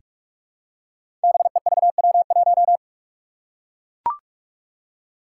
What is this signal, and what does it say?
Morse code, heard as a keyed tone, about 44 wpm, 700 Hz. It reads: BEVW1
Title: Five Challenging Letter-Number Combinations 45wpm